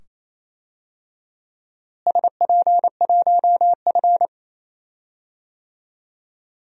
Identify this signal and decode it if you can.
Morse code, heard as a keyed tone, about 28 wpm, 700 Hz. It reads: SP1F
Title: DXCC-100 Call Signs - 4 Characters - Repeated 2x - 28wpm